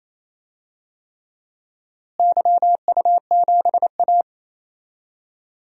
Morse code, heard as a keyed tone, about 28 wpm, 700 Hz. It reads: YU7A
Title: DXCC-100 Call Signs - 4 Characters - Repeated 3x - 28wpm